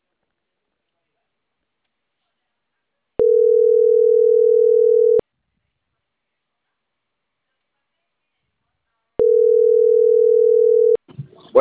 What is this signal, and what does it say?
Ringback tone (North America), 440 + 480 Hz.